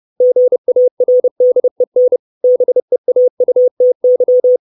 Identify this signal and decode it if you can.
Morse code, heard as a keyed tone, about 30 wpm, 500 Hz. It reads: GARDEN BEAUTY
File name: 632 words / 632 Words 30WPM